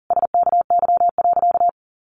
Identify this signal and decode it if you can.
Morse code, heard as a keyed tone, about 40 wpm, 700 Hz. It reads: SKY.
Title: A Sound Of Thunder 40wpm